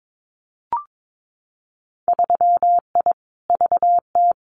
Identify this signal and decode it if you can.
Morse code, heard as a keyed tone, about 22 wpm, 700 Hz. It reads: 3I VT
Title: Winter Field Day Exchanges Encore 22wpm